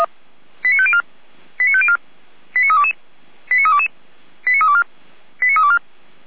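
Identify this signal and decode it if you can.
Five-tone selective call (ZVEI), heard as quick runs of five tones, tones of about 70 ms. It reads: 89504 89504 89322 89322 89324 89324
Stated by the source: ZVEI five-tone selective call received on an SDR.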